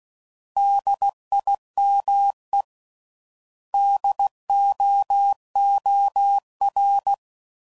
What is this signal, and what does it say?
Morse code, heard as a keyed tone, about 16 wpm, 800 Hz. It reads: DIME DOOR
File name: Prefixes / By First Letter / Ds